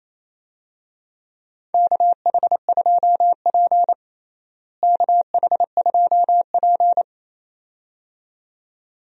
Morse code, heard as a keyed tone, about 28 wpm, 700 Hz. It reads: KH2P KH2P
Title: USA Call Signs - 4 Characters - Repeated 2x - 28wpm